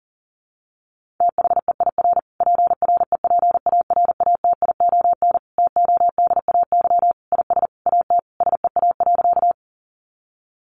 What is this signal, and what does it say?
Morse code, heard as a keyed tone, about 40 wpm, 700 Hz. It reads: THEIR PREPARATION TODAY IS AT SEA.
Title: Sentences from Top 1400 Words 40wpm